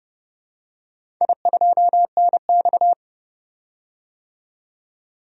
Morse code, heard as a keyed tone, about 29 wpm, 700 Hz. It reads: I2NX
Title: DXCC-100 Call Signs - 4 Characters - 30wpm